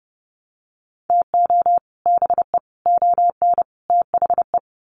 Morse code, heard as a keyed tone, about 30 wpm, 700 Hz. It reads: TO BE ON THE
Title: Sets of 4 Words 30wpm